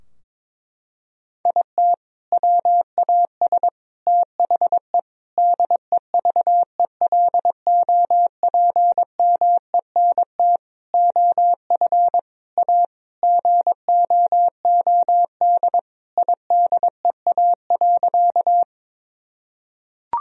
Morse code, heard as a keyed tone, about 22 wpm, 700 Hz. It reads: IT WAS THE DEVELOPMENT OF A GOOD IDEA.
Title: Sentences from Top 1000 Words - Review Part 4 - 22wpm